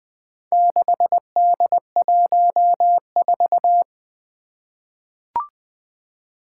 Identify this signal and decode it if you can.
Morse code, heard as a keyed tone, about 20 wpm, 700 Hz. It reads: 6D14
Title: Four Challenging Letter-Number Combinations 20wpm